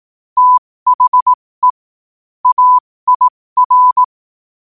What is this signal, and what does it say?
Morse code, heard as a keyed tone, about 18 wpm, spaced as for 15 wpm, 1000 Hz. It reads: THE AIR
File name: qotdfiles / 15wpm